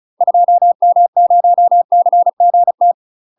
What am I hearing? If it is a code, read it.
Morse code, heard as a keyed tone, about 36 wpm, 700 Hz. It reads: 2M0CGT
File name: UK MASTER QSOs TRAINING/35WPM